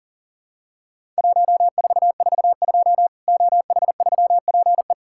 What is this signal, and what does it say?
Morse code, heard as a keyed tone, about 39 wpm, 700 Hz. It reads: 1442 OH3PE
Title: Worked All Europe DX Contest - QTC Exchange - 40wpm